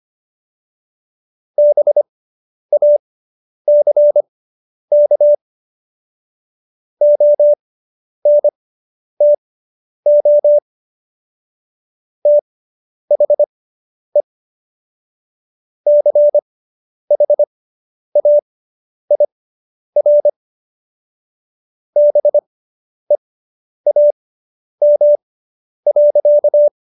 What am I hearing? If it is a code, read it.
Morse code, heard as a keyed tone, about 25 wpm, spaced as for 10 wpm, 590 Hz. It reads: BACK ONTO THE CHAIR BEAM.